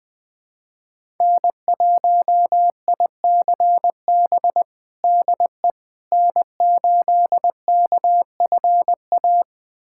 Morse code, heard as a keyed tone, about 21 wpm, 700 Hz. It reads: N1ICB DE N8KFA
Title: USA Callsign DE Callsign - 5 Characters - 20wpm